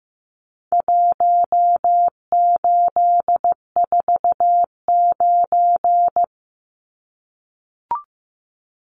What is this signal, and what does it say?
Morse code, heard as a keyed tone, about 15 wpm, 700 Hz. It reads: 1849
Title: Numbers - Contest Style - 4-Digit 15wpm